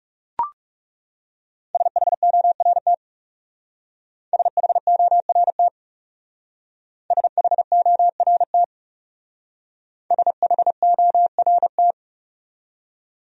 Morse code, heard as a keyed tone, about 37 wpm, 700 Hz. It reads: SHORT SHORT SHORT SHORT
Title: Speed-Racing Top 500 Words 30wpm